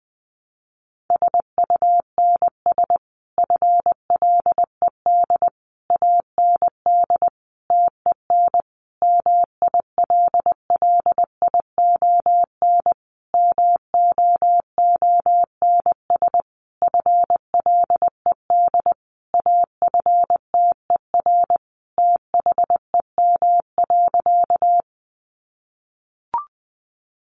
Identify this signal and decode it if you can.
Morse code, heard as a keyed tone, about 20 wpm, 700 Hz. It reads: SUNS FLED AND TEN MILLION MOONS FLED AFTER THEM.
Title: A Sound Of Thunder 20wpm